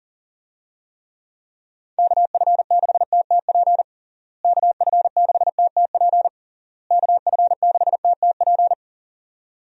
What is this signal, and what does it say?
Morse code, heard as a keyed tone, about 40 wpm, 700 Hz. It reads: KF6TTP KF6TTP KF6TTP
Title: USA Call Signs - 6 Characters - Repeated 3x - 40wpm